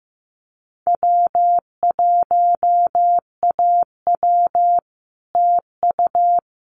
Morse code, heard as a keyed tone, about 15 wpm, 700 Hz. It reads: W1AW TU
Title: Common QSO Phrases Encore 15wpm